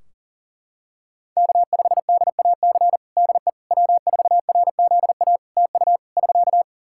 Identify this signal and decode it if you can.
Morse code, heard as a keyed tone, about 41 wpm, 700 Hz. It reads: K5DAC DE W4RZA TU <SK>
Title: Practice QSO - Exchange 4 - Ending - 40wpm